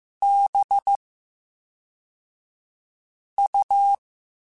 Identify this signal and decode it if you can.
Morse code, heard as a keyed tone, about 15 wpm, 800 Hz. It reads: B U
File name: Special INGs / Special INGs-mp3s / VerySlowToFaster Ing @10